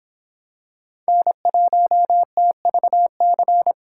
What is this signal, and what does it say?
Morse code, heard as a keyed tone, about 26 wpm, 700 Hz. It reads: N1TVC